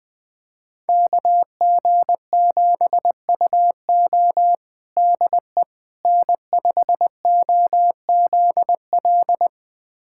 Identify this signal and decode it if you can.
Morse code, heard as a keyed tone, about 21 wpm, 700 Hz. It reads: KG7UO DE N5OZL
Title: USA Callsign DE Callsign - 5 Characters - 20wpm